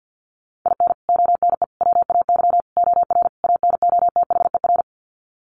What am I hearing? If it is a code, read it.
Morse code, heard as a keyed tone, about 50 wpm, 700 Hz. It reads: IN ONE WAY OR ANOTHER
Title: Sets of 5 Words Encore 50wpm